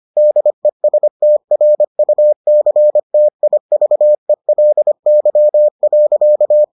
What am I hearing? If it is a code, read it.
Morse code, heard as a keyed tone, about 25 wpm, 600 Hz. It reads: DESTRUCTIVELY.